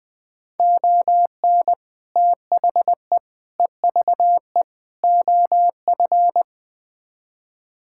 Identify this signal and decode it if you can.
Morse code, heard as a keyed tone, about 21 wpm, 700 Hz. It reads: ON THE EVE OF
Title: Build A Phrase - 5-Word Phrases - 20wpm